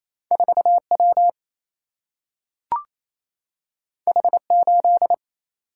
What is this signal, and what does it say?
Morse code, heard as a keyed tone, about 28 wpm, 700 Hz. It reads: 4W H8
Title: Single-Letter Single-Number 28wpm